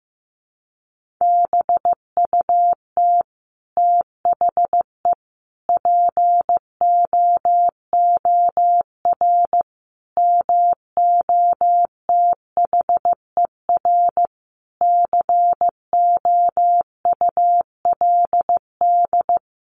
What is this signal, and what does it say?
Morse code, heard as a keyed tone, about 15 wpm, 700 Hz. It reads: BUT THE POOR MOTHER COULD
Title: Sentences from Top 1000 Words 15wpm